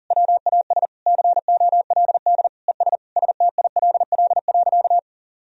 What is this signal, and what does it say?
Morse code, heard as a keyed tone, about 40 wpm, 700 Hz. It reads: WAS COLD ES STILL.
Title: Sentences from Top 400 Words and Abbreviations 40wpm